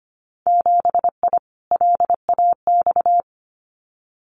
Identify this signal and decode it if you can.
Morse code, heard as a keyed tone, about 25 wpm, 700 Hz. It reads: 7I LAX
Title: Winter Field Day Exchanges Encore 25wpm